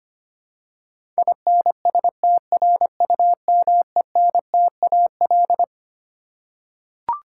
Morse code, heard as a keyed tone, about 25 wpm, 700 Hz. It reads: INSTRUMENTAL
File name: Twelve Letter Words - 25wpm